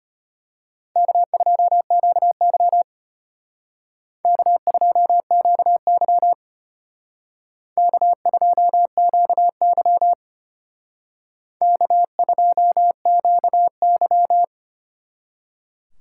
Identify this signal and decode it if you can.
Morse code, heard as a keyed tone, about 32 wpm, 700 Hz. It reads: K2QY K2QY K2QY K2QY